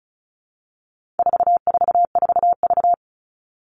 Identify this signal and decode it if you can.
Morse code, heard as a keyed tone, about 34 wpm, 700 Hz. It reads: 444V
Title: All Cut Numbers - 4-Digit 35wpm